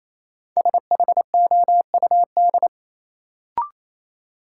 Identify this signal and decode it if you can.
Morse code, heard as a keyed tone, about 28 wpm, 700 Hz. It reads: SHOUD
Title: Sentences from Top 1000 Words - Review Part 4 - New Word Drill - 28wpm